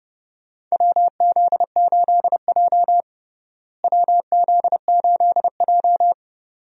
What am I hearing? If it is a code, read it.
Morse code, heard as a keyed tone, about 30 wpm, 700 Hz. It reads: WZ8J WZ8J
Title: USA Call Signs - 4 Characters - Repeated 2x - 30wpm